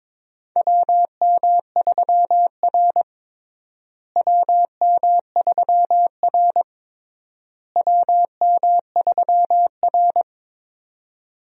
Morse code, heard as a keyed tone, about 22 wpm, 700 Hz. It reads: WM3R WM3R WM3R
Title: USA Call Signs - 4 Characters - Repeated 3x - 22wpm